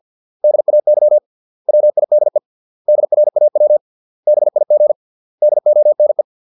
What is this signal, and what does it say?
Morse code, heard as a keyed tone, about 50 wpm, 600 Hz. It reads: DAY WIDE DRAW BIG DONE